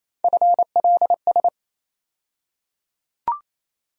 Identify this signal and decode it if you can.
Morse code, heard as a keyed tone, about 28 wpm, 700 Hz. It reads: FLS
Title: USA Call Sign Postfixes - 3 Letters 28wpm